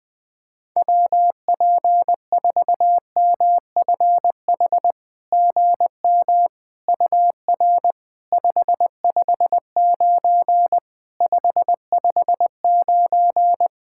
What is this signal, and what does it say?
Morse code, heard as a keyed tone, about 20 wpm, 700 Hz. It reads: WP4MFH GM UR 559 559
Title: SOTA QSO Practice 20wpm